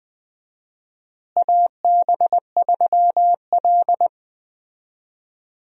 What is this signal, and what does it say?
Morse code, heard as a keyed tone, about 20 wpm, 700 Hz. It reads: AB3L
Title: USA Call Signs - 4 Characters - Repeated 2x - 20wpm